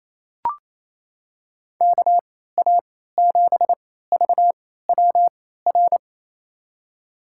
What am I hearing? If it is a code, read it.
Morse code, heard as a keyed tone, about 28 wpm, spaced as for 16 wpm, 700 Hz. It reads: KA7VWR
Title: USA Call Signs - 3x Character Spacing - 28wpm